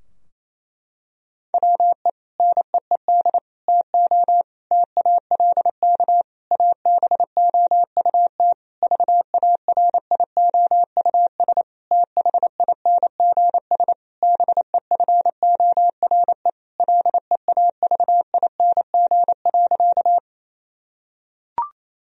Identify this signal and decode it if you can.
Morse code, heard as a keyed tone, about 28 wpm, 700 Hz. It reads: WE NEED TO TALK ABOUT VARIOUS THINGS BEFORE LEAVING.
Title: Sentences from Top 1000 Words - Review Part 4 - 28wpm